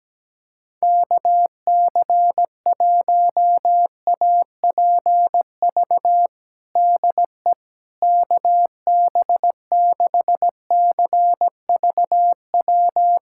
Morse code, heard as a keyed tone, about 17 wpm, 700 Hz. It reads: KC1APV DE KB6CVW
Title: USA Callsign DE Callsign 17wpm